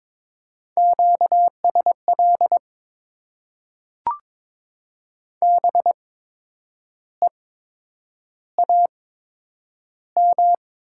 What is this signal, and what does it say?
Morse code, heard as a keyed tone, about 22 wpm, spaced as for 8 wpm, 700 Hz. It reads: QSL BEAM
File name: Top 90 QSO Elements - 8x Character Spacing - 22wpm